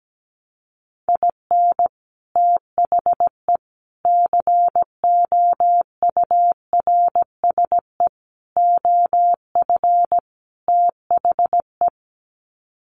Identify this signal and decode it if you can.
Morse code, heard as a keyed tone, about 17 wpm, 700 Hz. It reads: IN THE COURSE OF THE
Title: Sets of 5 Words 17wpm